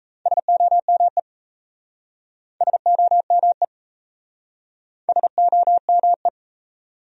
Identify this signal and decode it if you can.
Morse code, heard as a keyed tone, about 38 wpm, 700 Hz. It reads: SOME SOME SOME